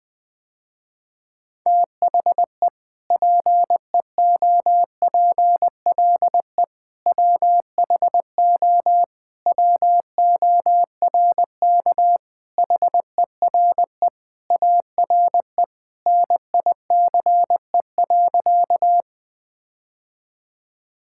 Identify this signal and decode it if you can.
Morse code, heard as a keyed tone, about 20 wpm, 700 Hz. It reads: THE PEOPLE WHO WORK HERE ARE NICE.
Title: Sentences with 3-Word Phrases - Spoken Beginning - 20wpm